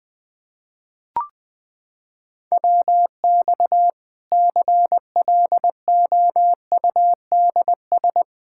Morse code, heard as a keyed tone, about 20 wpm, 700 Hz. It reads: WX CLOUDS
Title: Practice QSO - Exchange 2 - Info - 20wpm